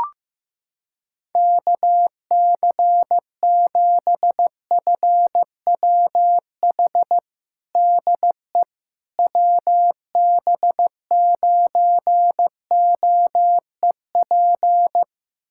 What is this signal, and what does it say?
Morse code, heard as a keyed tone, about 15 wpm, 700 Hz. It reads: KC7FWH DE WB9OEP
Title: USA Callsign DE Callsign - 6 Characters - 15wpm